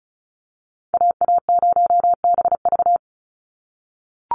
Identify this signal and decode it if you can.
Morse code, heard as a keyed tone, about 34 wpm, 700 Hz. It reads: AA0BV
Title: NA SOTA Activator Callsigns 35wpm